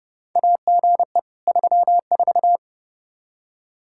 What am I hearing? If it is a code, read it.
Morse code, heard as a keyed tone, about 30 wpm, 700 Hz. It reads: AGE 34
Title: Practice QSO - Exchange 3 - More Info - 30wpm